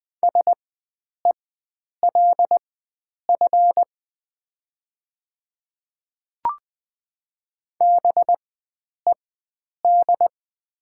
Morse code, heard as a keyed tone, about 20 wpm, spaced as for 9 wpm, 700 Hz. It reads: SELF BED
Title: 4x Character Spacing - Top 500 Words - 20wpm